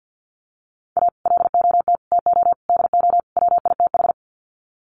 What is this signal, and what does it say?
Morse code, heard as a keyed tone, about 50 wpm, 700 Hz. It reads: A LOT TO DO WITH
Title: Sets of 5 Words 50wpm